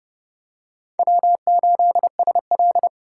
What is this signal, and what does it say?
Morse code, heard as a keyed tone, about 29 wpm, 700 Hz. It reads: W8SL